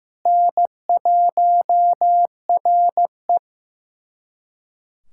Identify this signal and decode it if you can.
Morse code, heard as a keyed tone, about 16 wpm, 700 Hz. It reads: N1RE
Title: USA Callsign DE Callsign - 4 Characters - 15wpm